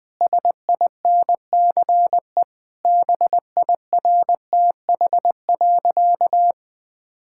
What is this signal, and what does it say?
Morse code, heard as a keyed tone, about 20 wpm, 700 Hz. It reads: SINCE BIRTH.
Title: Sentences from Top 1000 Words - Review Part 6 - 20wpm